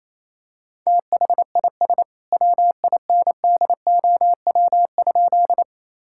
Morse code, heard as a keyed tone, about 28 wpm, 700 Hz. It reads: THIS WINDOW?
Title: Sentences from Top 1600 Words 28wpm